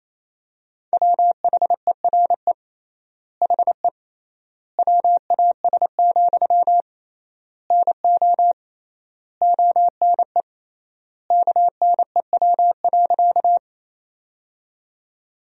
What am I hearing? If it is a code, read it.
Morse code, heard as a keyed tone, about 28 wpm, 700 Hz. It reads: WHERE HE WAS, NO ONE KNEW.
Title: 3x Word-Spacing - Sentences from Top 400 Words - 28wpm